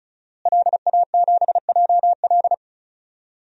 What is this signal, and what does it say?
Morse code, heard as a keyed tone, about 36 wpm, 700 Hz. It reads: LA7JL
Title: DXCC-100 Call Signs - 5 Characters - Repeated 3x - 35wpm